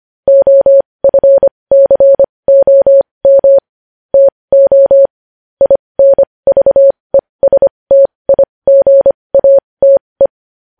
Morse code, heard as a keyed tone, about 25 wpm, spaced as for 20 wpm, 550 Hz. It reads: OFCOM TO INVESTIGATE
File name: podcast / News headlines-1x-25x20